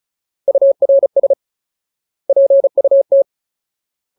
Morse code, heard as a keyed tone, about 35 wpm, 550 Hz. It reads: URS PUT
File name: Ultimate Word List / 35WPM Ultimate